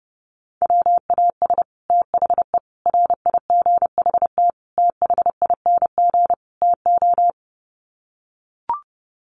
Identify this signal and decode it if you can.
Morse code, heard as a keyed tone, about 28 wpm, 700 Hz. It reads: WAS THE RIGHT THING TO
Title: Sets of 5 Words Encore2 30wpm